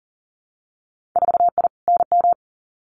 Morse code, heard as a keyed tone, about 39 wpm, 700 Hz. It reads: 4I NM
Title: Winter Field Day Exchanges Encore 40wpm